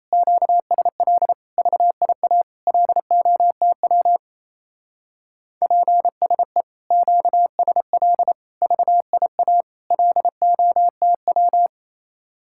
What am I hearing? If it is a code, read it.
Morse code, heard as a keyed tone, about 34 wpm, 700 Hz. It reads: QSL VIA LOTW PSE QSL VIA LOTW
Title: Speed-Racing Common QSO Phrases - 28wpm